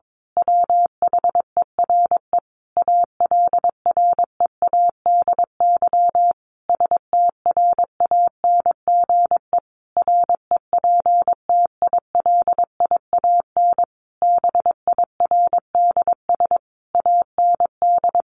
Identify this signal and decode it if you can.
Morse code, heard as a keyed tone, about 22 wpm, 700 Hz. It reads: WHERE ALREADY STRANGE REPTILIAN BIRDS AND
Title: A Sound Of Thunder 22wpm